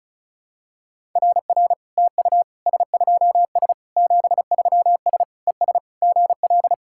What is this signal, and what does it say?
Morse code, heard as a keyed tone, about 36 wpm, 700 Hz. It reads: RR TU S2S 73S ES GL
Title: SOTA QSO Practice 35wpm